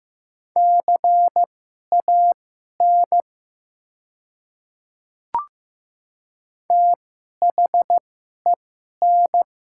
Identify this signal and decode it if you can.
Morse code, heard as a keyed tone, about 15 wpm, spaced as for 11 wpm, 700 Hz. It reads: CAN THEN
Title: 2x Character Spacing - Top 100 Words - 15wpm